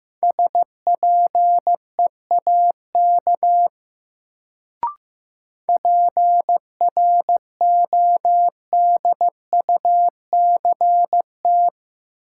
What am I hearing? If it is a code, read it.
Morse code, heard as a keyed tone, about 15 wpm, 700 Hz. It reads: SPEAK PRODUCT
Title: Sentences from Top 1000 Words - Review Part 2 - New Word Drill - 15wpm